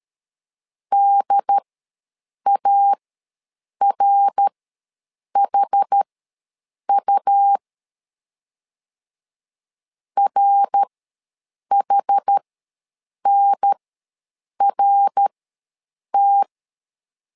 Morse code, heard as a keyed tone, about 13 wpm, spaced as for 7 wpm, 800 Hz. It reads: DARHU RHNRT
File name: LEZIONI / 3 MWRO / 3 MWR ws 12wpm 1809